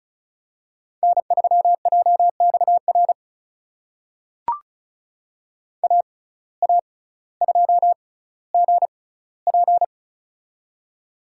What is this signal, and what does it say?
Morse code, heard as a keyed tone, about 36 wpm, spaced as for 18 wpm, 700 Hz. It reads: N3JXR AA2GP
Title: USA Call Signs - 5 Characters - 6x Character Spacing - 35wpm